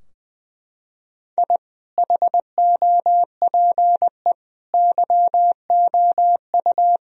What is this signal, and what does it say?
Morse code, heard as a keyed tone, about 20 wpm, 700 Hz. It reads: I HOPE YOU
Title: Sentences from Top 1000 Words - Review Part 2 - 20wpm